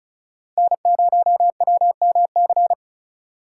Morse code, heard as a keyed tone, about 36 wpm, 700 Hz. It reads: N0WMC
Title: USA Call Signs - 5 Characters - 2x Character Spacing - 35wpm